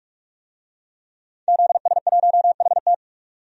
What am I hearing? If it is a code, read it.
Morse code, heard as a keyed tone, about 48 wpm, 700 Hz. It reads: ZS1HT
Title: DXCC-100 Call Signs - 5 Characters - 45wpm